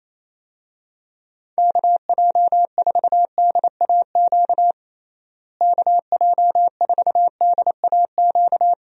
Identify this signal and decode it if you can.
Morse code, heard as a keyed tone, about 27 wpm, 700 Hz. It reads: KJ4DAQ KJ4DAQ
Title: USA Call Signs - 6 Characters - Repeated 3x - 28wpm